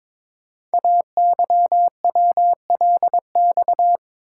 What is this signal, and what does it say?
Morse code, heard as a keyed tone, about 22 wpm, 700 Hz. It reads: AYWLX
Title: Code Groups - 5 Letters - 22wpm